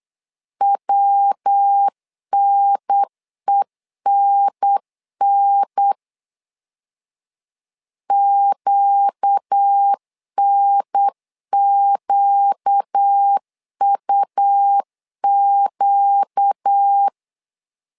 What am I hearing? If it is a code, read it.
Morse code, heard as a keyed tone, about 8 wpm, 800 Hz. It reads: WNENN QNQUQ
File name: 4 Q 8wpm